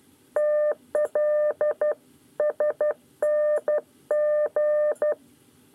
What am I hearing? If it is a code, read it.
Morse code, heard as a keyed tone, about 12 wpm, 600 Hz. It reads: TLSNG
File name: 9 lezione cw 2